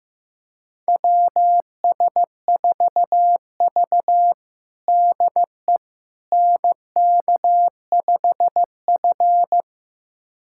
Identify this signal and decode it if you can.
Morse code, heard as a keyed tone, about 15 wpm, 700 Hz. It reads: WS4V DE NK5F
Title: USA Callsign DE Callsign - 4 Characters - 15wpm